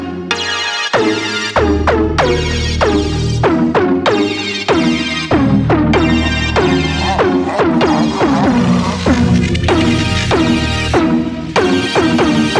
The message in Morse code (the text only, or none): none